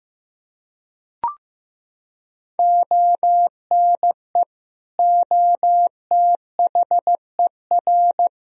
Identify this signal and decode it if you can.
Morse code, heard as a keyed tone, about 16 wpm, 700 Hz. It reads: ONE OTHER
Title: Sentences from Top 400 Words and Abbreviations 15wpm